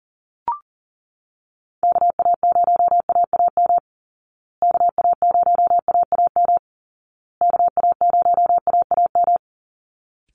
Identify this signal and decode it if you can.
Morse code, heard as a keyed tone, about 40 wpm, 700 Hz. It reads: KA0AAM KA0AAM KA0AAM
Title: Call Signs Repeated 2x - 40wpm